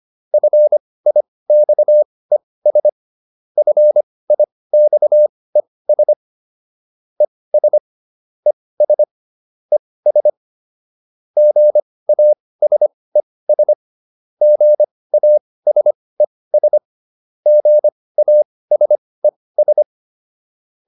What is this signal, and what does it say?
Morse code, heard as a keyed tone, about 25 wpm, spaced as for 18 wpm, 600 Hz. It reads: FIXES FIXES ES ES ES GASES GASES GASES